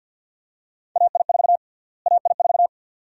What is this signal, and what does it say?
Morse code, heard as a keyed tone, about 51 wpm, 700 Hz. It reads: AI4 AI4
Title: USA Call Sign Prefixes - Repeated 2x - 50wpm